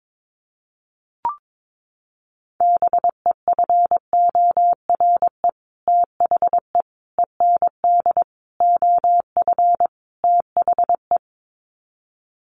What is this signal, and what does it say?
Morse code, heard as a keyed tone, about 22 wpm, 700 Hz. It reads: BEFORE THE END OF THE
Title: Sets of 5 Words Encore 22wpm